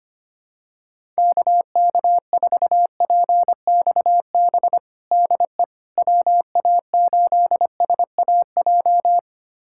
Morse code, heard as a keyed tone, about 25 wpm, 700 Hz. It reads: KK4PXB DE WA8SAJ
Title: USA Callsign DE Callsign - 6 Characters - 25wpm